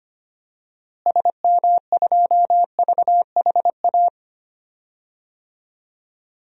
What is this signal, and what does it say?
Morse code, heard as a keyed tone, about 25 wpm, 700 Hz. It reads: SM2VHA
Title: DXCC-100 Call Signs - 6 Characters - Repeated 2x - 25wpm